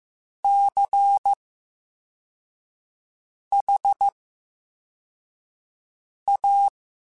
Morse code, heard as a keyed tone, about 15 wpm, spaced as for 4 wpm, 800 Hz. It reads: CHA
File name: Special INGs / Special INGs-mp3s / Slow ing @13